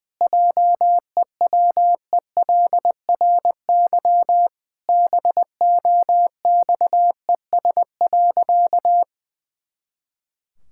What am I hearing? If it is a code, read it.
Morse code, heard as a keyed tone, about 19 wpm, 700 Hz. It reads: JEWELRY BOXES.
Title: Pangram Challenge v2 - 20wpm